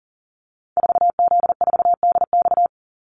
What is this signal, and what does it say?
Morse code, heard as a keyed tone, about 40 wpm, 700 Hz. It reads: 4Z4DX
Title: Call Signs Repeated 3x - 40wpm